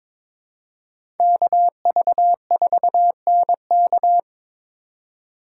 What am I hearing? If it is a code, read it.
Morse code, heard as a keyed tone, about 21 wpm, 700 Hz. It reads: KV4NK
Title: USA Call Signs - 5 Characters - 4x Character Spacing - 22wpm